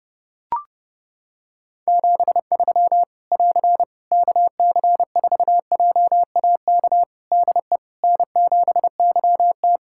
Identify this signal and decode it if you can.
Morse code, heard as a keyed tone, about 28 wpm, 700 Hz. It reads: 73 <AR> KC4JAK DE N7YT
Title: Practice QSO - Exchange 4 - Ending - 30wpm